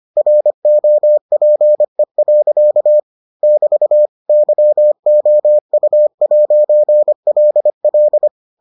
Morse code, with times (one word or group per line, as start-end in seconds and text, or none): 0.2-3.0 ROPE.
3.4-4.1 <BT>
4.3-8.3 YOU'LL